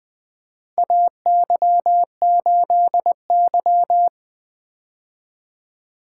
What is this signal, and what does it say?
Morse code, heard as a keyed tone, about 24 wpm, 700 Hz. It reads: AY8Y
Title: Speed-Racing - Four Letter-Number Combinations - 20wpm